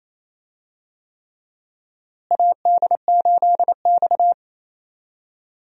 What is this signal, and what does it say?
Morse code, heard as a keyed tone, about 28 wpm, 700 Hz. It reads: AD8X
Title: USA Call Signs - 4 Characters - Repeated 3x - 28wpm